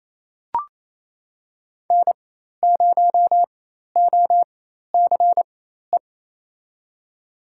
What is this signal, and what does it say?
Morse code, heard as a keyed tone, about 28 wpm, spaced as for 13 wpm, 700 Hz. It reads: N0OCE
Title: USA Call Signs - 4x Character Spacing - 28wpm